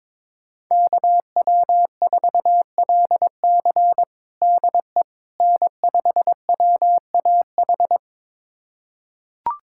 Morse code, heard as a keyed tone, about 22 wpm, 700 Hz. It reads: KW4LC DE N5WAH
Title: USA Callsign DE Callsign - 5 Characters - 22wpm